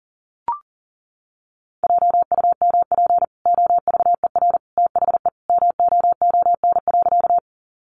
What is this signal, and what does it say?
Morse code, heard as a keyed tone, about 40 wpm, 700 Hz. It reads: JUMP OVER THE MOON.
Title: Sentences from Top 1300 Words 40wpm